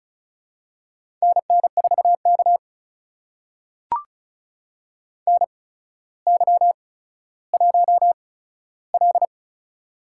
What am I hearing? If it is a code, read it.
Morse code, heard as a keyed tone, about 36 wpm, spaced as for 15 wpm, 700 Hz. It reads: NN4K NY1L
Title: USA Call Signs - 4 Characters - 8x Character Spacing - 35wpm